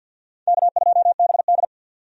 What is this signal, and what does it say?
Morse code, heard as a keyed tone, about 50 wpm, 700 Hz. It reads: K2BD
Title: USA Call Signs - 4 Characters - 50wpm